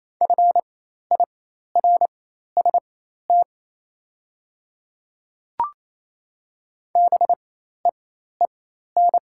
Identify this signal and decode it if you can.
Morse code, heard as a keyed tone, about 28 wpm, spaced as for 13 wpm, 700 Hz. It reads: FIRST BEEN
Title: 4x Character Spacing - Top 100 Words - 28wpm